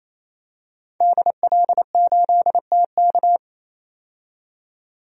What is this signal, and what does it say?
Morse code, heard as a keyed tone, about 28 wpm, 700 Hz. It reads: DL8TK
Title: DXCC-100 Call Signs - 5 Characters - 28wpm